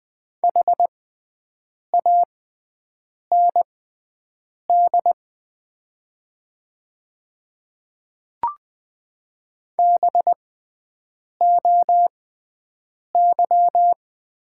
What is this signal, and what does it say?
Morse code, heard as a keyed tone, about 20 wpm, spaced as for 7 wpm, 700 Hz. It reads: HAND BOY
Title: 6x Character Spacing - Top 200 Words - 20wpm